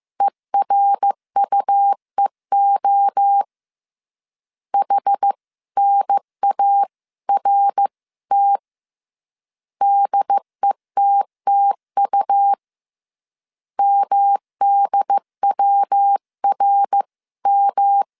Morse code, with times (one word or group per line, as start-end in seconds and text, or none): 0.2-3.4 ERUEO
4.7-8.6 HNART
9.8-12.5 DETTU
13.8-18.0 MDWRM